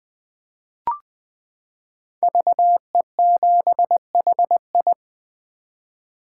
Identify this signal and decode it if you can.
Morse code, heard as a keyed tone, about 20 wpm, 700 Hz. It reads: VE7HI
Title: 2023 SOTA Chaser Call Signs - 20wpm